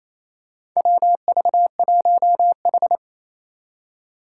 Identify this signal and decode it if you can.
Morse code, heard as a keyed tone, about 29 wpm, 700 Hz. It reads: WV1H
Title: Four Challenging Letter-Number Combinations 28wpm